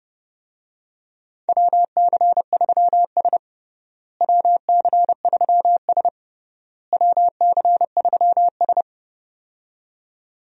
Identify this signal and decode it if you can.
Morse code, heard as a keyed tone, about 30 wpm, 700 Hz. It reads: WC3S WC3S WC3S
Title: USA Call Signs - 4 Characters - Repeated 3x - 30wpm